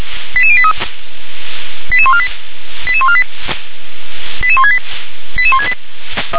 Five-tone selective call (ZVEI), heard as quick runs of five tones, tones of about 70 ms. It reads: 88083 88148 88148 88168 88168